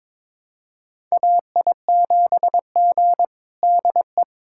Morse code, heard as a keyed tone, about 22 wpm, 700 Hz. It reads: AI7G DE
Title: USA Callsign DE Callsign - 4 Characters - 22wpm